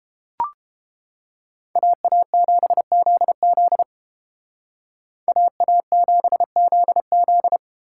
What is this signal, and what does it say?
Morse code, heard as a keyed tone, about 31 wpm, 700 Hz. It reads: AA7ZZ AA7ZZ